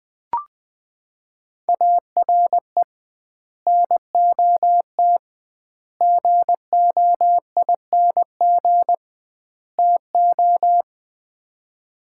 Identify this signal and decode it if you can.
Morse code, heard as a keyed tone, about 20 wpm, 700 Hz. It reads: ARE NOT GOING TO
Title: 2x Word-Spacing - Sets of 4 Words - 20wpm